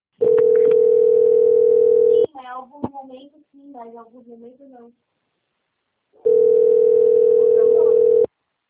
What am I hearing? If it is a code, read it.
Ringback tone (North America), 440 + 480 Hz, with speech.
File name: out- - - - - .3308